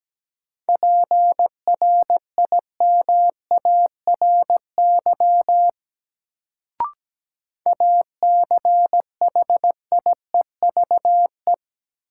Morse code, heard as a keyed tone, about 17 wpm, 700 Hz. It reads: PRIMARY ACHIEVE
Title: Sentences from Top 1000 Words - Review Part 5 - New Word Drill - 17wpm